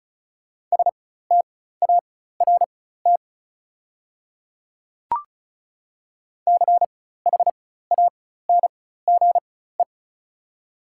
Morse code, heard as a keyed tone, about 34 wpm, spaced as for 16 wpm, 690 Hz. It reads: START CHANGE
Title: 4x Character Spacing - Top 300 Words - 35wpm